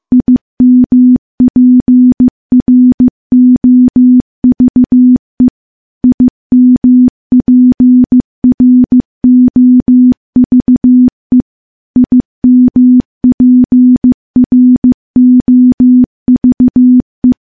Morse code, heard as a keyed tone, about 15 wpm, 250 Hz. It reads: IMPROVE IMPROVE IMPROVE
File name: morse / morse-training-15WPM